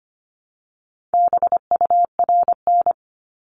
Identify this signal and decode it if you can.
Morse code, heard as a keyed tone, about 25 wpm, 700 Hz. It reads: BURN
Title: Multi-letter Phonemes Repeated 3x 25wpm